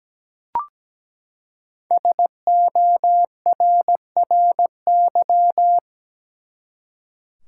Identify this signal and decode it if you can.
Morse code, heard as a keyed tone, about 17 wpm, 700 Hz. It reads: SORRY
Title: Five Letter Words - 17wpm